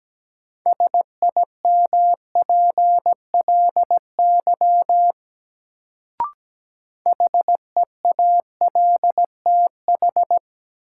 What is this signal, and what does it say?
Morse code, heard as a keyed tone, about 17 wpm, 700 Hz. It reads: SIMPLY HEALTH
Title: Sentences from Top 600 Words - New Word Drill - 17wpm